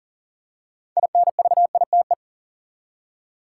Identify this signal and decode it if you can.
Morse code, heard as a keyed tone, about 37 wpm, 700 Hz. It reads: INVITE
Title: Six Letter Words - 40wpm